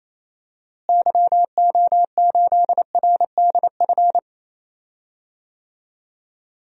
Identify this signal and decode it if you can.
Morse code, heard as a keyed tone, about 28 wpm, 700 Hz. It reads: YO8RDF
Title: DXCC-100 Call Signs - 6 Characters - Repeated 2x - 28wpm